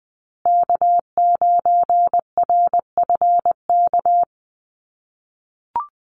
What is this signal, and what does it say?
Morse code, heard as a keyed tone, about 20 wpm, 700 Hz. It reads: K9RFK
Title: USA Call Signs Repeated 3x 20wpm